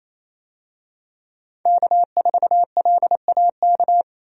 Morse code, heard as a keyed tone, about 28 wpm, 700 Hz. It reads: K4LAK
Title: USA Call Signs - 5 Characters - 28wpm